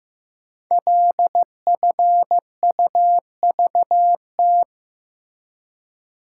Morse code, heard as a keyed tone, about 15 wpm, 700 Hz. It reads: LFUVT
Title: Code Groups - 5 Letters - 15wpm